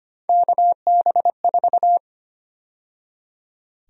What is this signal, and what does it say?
Morse code, heard as a keyed tone, about 25 wpm, 700 Hz. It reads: KB4
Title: USA Call Sign Prefixes - 25wpm